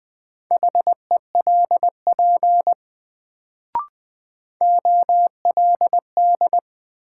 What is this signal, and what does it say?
Morse code, heard as a keyed tone, about 20 wpm, 700 Hz. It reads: HELP OLD
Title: Sets of 2 Words - New Word Drill - 20wpm